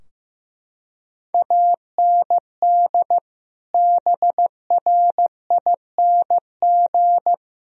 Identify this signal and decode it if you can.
Morse code, heard as a keyed tone, about 15 wpm, 700 Hz. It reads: AND BRING
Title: Sentences from Top 1000 Words - Review Part 3 - 15wpm